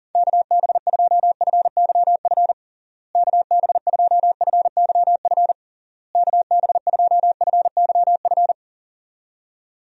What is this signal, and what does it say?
Morse code, heard as a keyed tone, about 40 wpm, 700 Hz. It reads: KB2FYF KB2FYF KB2FYF
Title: USA Call Signs - 6 Characters - Repeated 3x - 40wpm